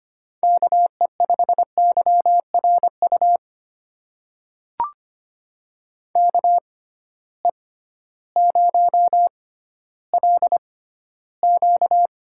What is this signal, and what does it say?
Morse code, heard as a keyed tone, about 25 wpm, spaced as for 13 wpm, 700 Hz. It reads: KE5YRU KE0LQ
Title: USA Call Signs - 6x Character Spacing - 25wpm